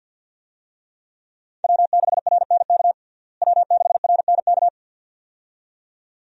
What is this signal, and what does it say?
Morse code, heard as a keyed tone, about 52 wpm, 700 Hz. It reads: W6RNK W6RNK
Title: USA Call Signs - 5 Characters - Repeated 2x - 50wpm